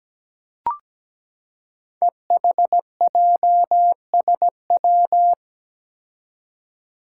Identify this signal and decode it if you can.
Morse code, heard as a keyed tone, about 17 wpm, 700 Hz. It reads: EHJSW
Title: Code Groups - 5 Letters - 17wpm